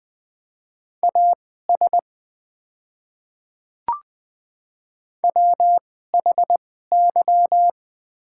Morse code, heard as a keyed tone, about 20 wpm, spaced as for 14 wpm, 700 Hz. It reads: AS WHY
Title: 2x Character Spacing - Top 300 Words - 20wpm